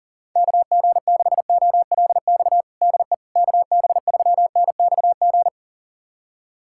Morse code, heard as a keyed tone, about 40 wpm, 700 Hz. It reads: KG6OLX DE KB3NXG
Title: USA Callsign DE Callsign - 6 Characters - 40wpm